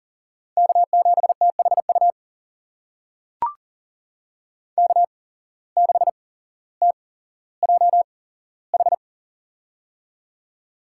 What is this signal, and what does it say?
Morse code, heard as a keyed tone, about 40 wpm, spaced as for 17 wpm, 700 Hz. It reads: K7THU K6TJH
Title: USA Call Signs - 5 Characters - 8x Character Spacing - 40wpm